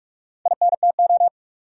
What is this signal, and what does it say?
Morse code, heard as a keyed tone, about 42 wpm, 700 Hz. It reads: INTO
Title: Sentences from Top 600 Words - New Word Drill - 45wpm